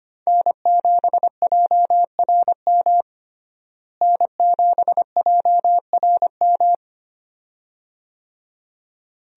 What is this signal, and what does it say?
Morse code, heard as a keyed tone, about 25 wpm, 700 Hz. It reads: N7JRM N7JRM
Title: USA Call Signs - 5 Characters - Repeated 2x - 25wpm